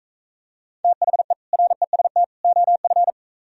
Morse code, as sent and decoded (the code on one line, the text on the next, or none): - .... / .-. ... - / --- ..-.
THE REST OF